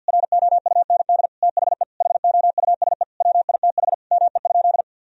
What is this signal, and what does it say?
Morse code, heard as a keyed tone, about 51 wpm, 700 Hz. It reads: ROUND THE HOUSE WITH ME?
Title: Sentences from Top 1000 Words - Review Part 1 - 50wpm